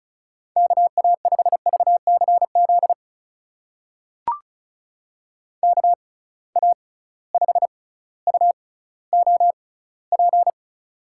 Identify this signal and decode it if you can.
Morse code, heard as a keyed tone, about 35 wpm, spaced as for 18 wpm, 700 Hz. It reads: KA5VCZ KA5UOP
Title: USA Call Signs - 6 Characters - 6x Character Spacing - 35wpm